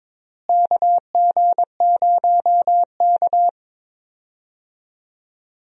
Morse code, heard as a keyed tone, about 22 wpm, 700 Hz. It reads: KG0K
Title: USA Call Signs - 4 Characters - Repeated 2x - 22wpm